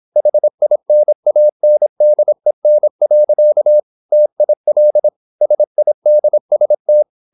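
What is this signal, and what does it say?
Morse code, heard as a keyed tone, about 26 wpm, 600 Hz. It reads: HINANDEN. TIL SIDST
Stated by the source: High-speed (QRQ) Morse at 26 wpm.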